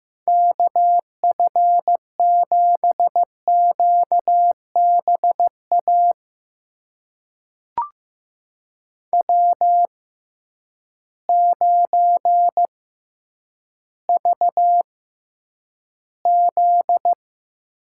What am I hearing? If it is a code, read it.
Morse code, heard as a keyed tone, about 15 wpm, spaced as for 9 wpm, 700 Hz. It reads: KF7QBA W9VZ
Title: USA Call Signs - 6x Character Spacing - 15wpm